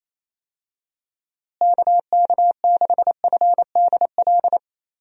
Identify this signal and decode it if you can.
Morse code, heard as a keyed tone, about 28 wpm, 700 Hz. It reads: KK6FDL
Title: USA Call Signs - 6 Characters - Repeated 3x - 28wpm